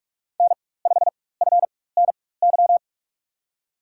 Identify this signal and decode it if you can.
Morse code, heard as a keyed tone, about 47 wpm, spaced as for 21 wpm, 700 Hz. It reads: N5FNY
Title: USA Call Signs - 5 Characters - 4x Character Spacing - 45wpm